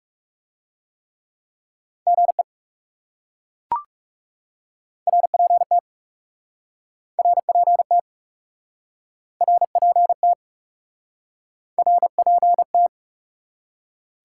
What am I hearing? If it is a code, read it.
Morse code, heard as a keyed tone, about 40 wpm, 700 Hz. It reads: GE RPT RPT RPT RPT